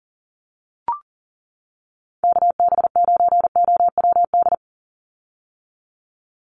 Morse code, heard as a keyed tone, about 40 wpm, 700 Hz. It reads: KB9OWD
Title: Call Signs - 40wpm